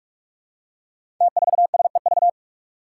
Morse code, heard as a keyed tone, about 46 wpm, 700 Hz. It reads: T4SEV
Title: Five Challenging Letter-Number Combinations 45wpm